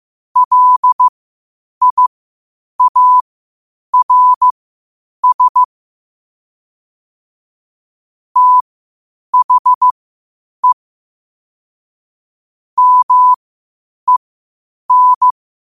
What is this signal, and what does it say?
Morse code, heard as a keyed tone, about 15 wpm, spaced as for 8 wpm, 1000 Hz. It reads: LIARS THE MEN
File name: qotdfiles / 8wpm